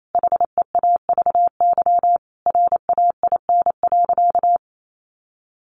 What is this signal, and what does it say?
Morse code, heard as a keyed tone, about 28 wpm, 700 Hz. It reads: HEAVY RAIN.
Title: Sentences from Top 1100 Words 28wpm